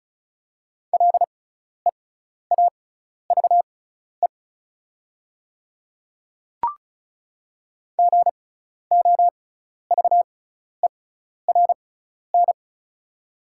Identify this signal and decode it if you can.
Morse code, heard as a keyed tone, about 36 wpm, spaced as for 12 wpm, 700 Hz. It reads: LEAVE GOVERN
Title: 6x Character Spacing - Top 500 Words - 35wpm